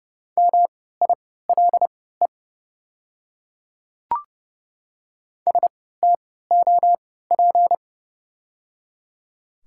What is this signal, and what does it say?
Morse code, heard as a keyed tone, about 30 wpm, spaced as for 17 wpm, 700 Hz. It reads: MILE STOP
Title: 3x Character Spacing - Top 300 Words - 30wpm